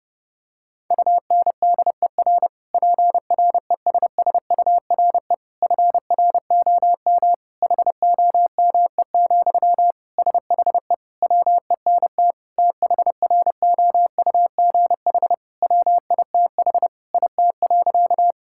Morse code, heard as a keyed tone, about 32 wpm, 700 Hz. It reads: UNDER PRESSURE FROM HOME, SHE WENT THROUGH WITH IT.
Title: Sentences from Top 1000 Words - Review Part 6 - 30wpm